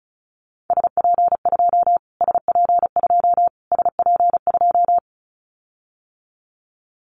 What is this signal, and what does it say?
Morse code, heard as a keyed tone, about 34 wpm, 700 Hz. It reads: SP2 SP2 SP2
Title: Callsign Prefixes for Top 25 DXCC - Repeated 3x - 35wpm